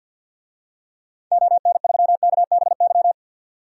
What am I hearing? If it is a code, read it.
Morse code, heard as a keyed tone, about 52 wpm, 700 Hz. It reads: ON3KBY
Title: DXCC-100 Call Signs - 6 Characters - 50wpm